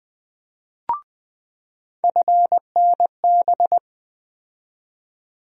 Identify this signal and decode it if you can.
Morse code, heard as a keyed tone, about 20 wpm, 700 Hz. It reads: FNB
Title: USA Call Sign Postfixes - 3 Letters 20wpm